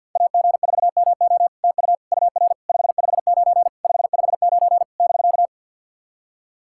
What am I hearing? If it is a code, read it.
Morse code, heard as a keyed tone, about 48 wpm, 700 Hz. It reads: AG4MO TU UR 559 559 <BK>
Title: SOTA QSO Practice 50wpm